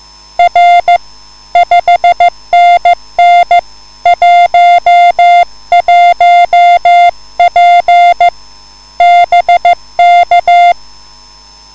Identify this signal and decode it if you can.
Morse code, heard as a keyed tone, about 14 wpm, 700 Hz. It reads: R 5NN 11P BK